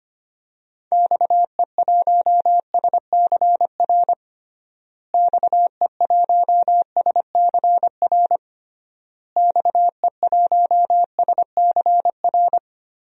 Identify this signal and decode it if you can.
Morse code, heard as a keyed tone, about 25 wpm, 700 Hz. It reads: XE1SCR XE1SCR XE1SCR
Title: DXCC-100 Call Signs - 6 Characters - Repeated 3x - 25wpm